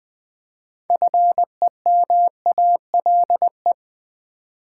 Morse code, heard as a keyed tone, about 20 wpm, 700 Hz. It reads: FEMALE
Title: 6 Letter Words - First 3 Letters Given - 20wpm